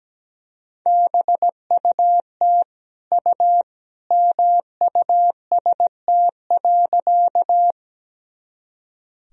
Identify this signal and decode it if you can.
Morse code, heard as a keyed tone, about 17 wpm, 700 Hz. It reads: BUT U MUST.
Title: Sentences from Top 200 Words and Abbreviations 17wpm